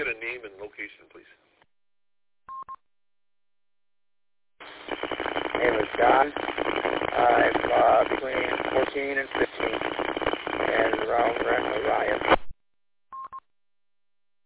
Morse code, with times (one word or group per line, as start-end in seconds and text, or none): none